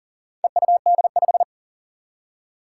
Morse code, heard as a keyed tone, about 42 wpm, 700 Hz. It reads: EUD5
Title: Four Challenging Letter-Number Combinations 40wpm